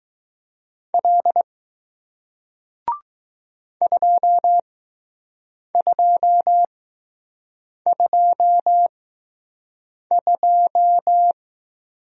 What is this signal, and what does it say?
Morse code, heard as a keyed tone, about 20 wpm, spaced as for 6 wpm, 700 Hz. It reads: L 2222